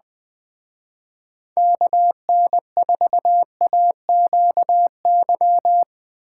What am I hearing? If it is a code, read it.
Morse code, heard as a keyed tone, about 24 wpm, 700 Hz. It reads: KN4AQY
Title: Speed Racing - USA Call Signs 20wpm